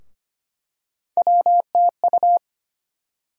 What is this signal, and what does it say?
Morse code, heard as a keyed tone, about 25 wpm, 700 Hz. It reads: WTU
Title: USA Call Sign Postfixes - 3 Letters 25wpm